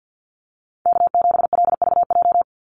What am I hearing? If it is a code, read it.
Morse code, heard as a keyed tone, about 49 wpm, 700 Hz. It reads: K7LVJ
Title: Call Signs - 50wpm